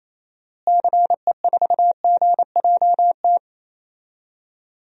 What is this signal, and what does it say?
Morse code, heard as a keyed tone, about 28 wpm, 700 Hz. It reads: CE4GJT
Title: DXCC-100 Call Signs - 6 Characters - Repeated 3x - 28wpm